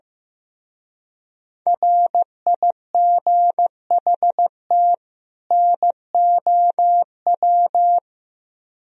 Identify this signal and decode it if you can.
Morse code, heard as a keyed tone, about 16 wpm, 700 Hz. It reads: RIGHT NOW
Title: Speed-Racing - Sets of 2 Words 15wpm